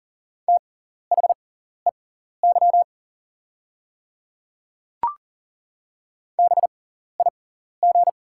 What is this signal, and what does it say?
Morse code, heard as a keyed tone, about 39 wpm, spaced as for 14 wpm, 690 Hz. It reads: THEY BIG
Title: 6x Character Spacing - Top 200 Words - 40wpm